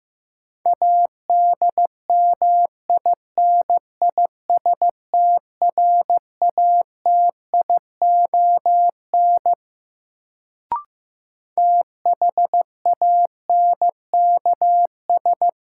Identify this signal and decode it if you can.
Morse code, heard as a keyed tone, about 15 wpm, 700 Hz. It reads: ADMINISTRATION THANKS
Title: Sets of 5 Words - New Word Drill - 15wpm